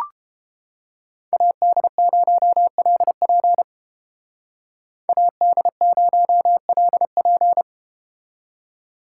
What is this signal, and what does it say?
Morse code, heard as a keyed tone, about 31 wpm, 700 Hz. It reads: AD0LP AD0LP